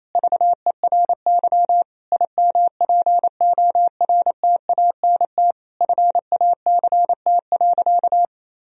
Morse code, heard as a keyed tone, about 28 wpm, 700 Hz. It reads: VERY IMPORTANT FACT.
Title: Sentences from Top 1000 Words - Review Part 3 - 28wpm